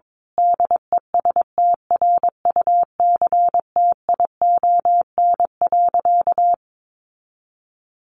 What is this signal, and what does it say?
Morse code, heard as a keyed tone, about 22 wpm, 700 Hz. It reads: DESTRUCTION.
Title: A Sound Of Thunder 22wpm